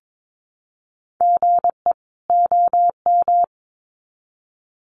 Morse code, heard as a keyed tone, about 22 wpm, 700 Hz. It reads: GE OM
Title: Common QSO Phrases 22wpm